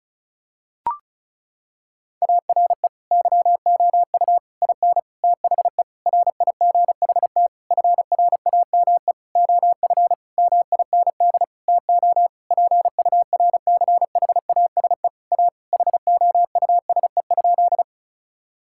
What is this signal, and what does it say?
Morse code, heard as a keyed tone, about 36 wpm, 700 Hz. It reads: ARE YOU IN THE RIGHT FRAME OF MIND TO PURCHASE A HOUSE?
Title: Sentences from Top 1000 Words - Review Part 6 - 35wpm